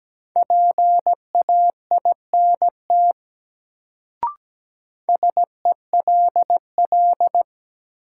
Morse code, heard as a keyed tone, about 17 wpm, 700 Hz. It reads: PAINT SELL
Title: Sentences from Top 1000 Words - Review Part 6 - New Word Drill - 17wpm